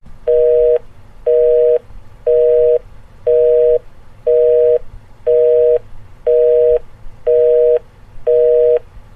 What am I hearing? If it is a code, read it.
Busy tone (North America), 480 + 620 Hz, with noise.
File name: effects-busy-signal